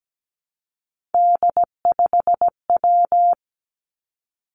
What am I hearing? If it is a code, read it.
Morse code, heard as a keyed tone, about 17 wpm, 700 Hz. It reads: D5W
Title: Three Challenging Letter-Number Combinations 17wpm